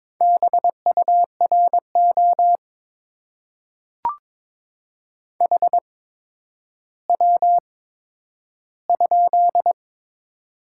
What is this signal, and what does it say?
Morse code, heard as a keyed tone, about 22 wpm, spaced as for 11 wpm, 700 Hz. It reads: BURO HW?
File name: Top 90 QSO Elements - 8x Character Spacing - 22wpm